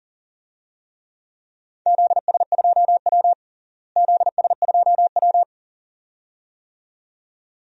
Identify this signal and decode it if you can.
Morse code, heard as a keyed tone, about 40 wpm, 700 Hz. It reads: ZS2W ZS2W
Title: DXCC-100 Call Signs - 4 Characters - Repeated 2x - 40wpm